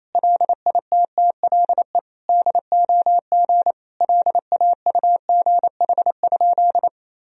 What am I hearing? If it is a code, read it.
Morse code, heard as a keyed tone, about 28 wpm, 700 Hz. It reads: LITTLE DOG LAUGH?
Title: Sentences from Top 500 Words and Abbreviations 28wpm